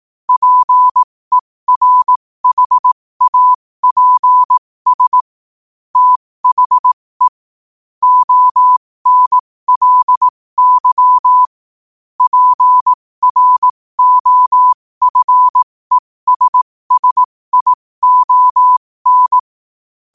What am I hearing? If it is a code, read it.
Morse code, heard as a keyed tone, about 18 wpm, spaced as for 15 wpm, 1000 Hz. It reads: PERHAPS THE ONLY PROFESSION
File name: qotdfiles / 15wpm